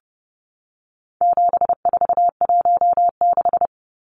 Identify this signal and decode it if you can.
Morse code, heard as a keyed tone, about 30 wpm, 700 Hz. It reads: 7416
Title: Numbers - 4-Digit 30wpm